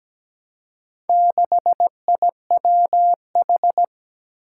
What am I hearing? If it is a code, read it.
Morse code, heard as a keyed tone, about 17 wpm, 700 Hz. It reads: 6IWH
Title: Four Challenging Letter-Number Combinations 17wpm